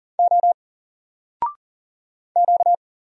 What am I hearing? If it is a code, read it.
Morse code, heard as a keyed tone, about 40 wpm, 700 Hz. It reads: O Q
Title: Single Letter-Number 40wpm